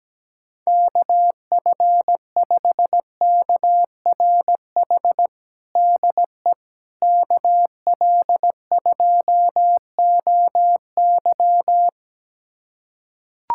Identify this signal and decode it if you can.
Morse code, heard as a keyed tone, about 17 wpm, 700 Hz. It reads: KF5KRH DE KL2OY
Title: USA Callsign DE Callsign 17wpm